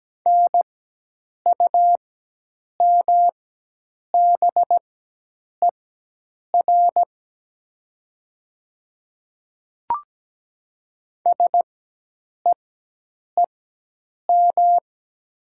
Morse code, heard as a keyed tone, about 17 wpm, spaced as for 8 wpm, 700 Hz. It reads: NUMBER SEEM
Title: 4x Character Spacing - Top 500 Words - 17wpm